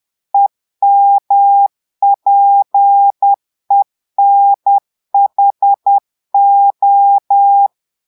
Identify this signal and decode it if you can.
Morse code, heard as a keyed tone, about 10 wpm, 800 Hz. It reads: EMPENHO
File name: Portugais - texte en clair 1 - vitesse - 10 mots minute